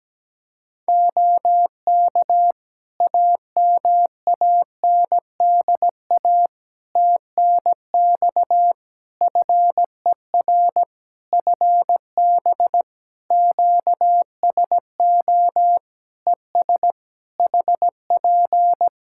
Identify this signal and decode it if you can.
Morse code, heard as a keyed tone, about 17 wpm, 700 Hz. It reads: OK AMANDA TNX FER FB QSO ES HP
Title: K1USN SST QSO Practice 17wpm